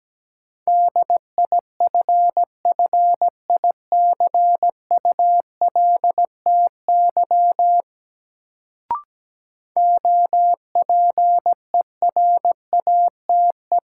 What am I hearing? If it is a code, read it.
Morse code, heard as a keyed tone, about 17 wpm, 700 Hz. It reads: DIFFICULTY OPERATE
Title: Sentences from Top 1500 Words - New Word Drill - 17wpm